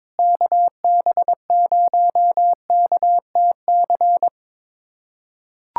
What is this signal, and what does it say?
Morse code, heard as a keyed tone, about 22 wpm, 700 Hz. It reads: KB0KTC
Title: USA Callsign DE Callsign 22wpm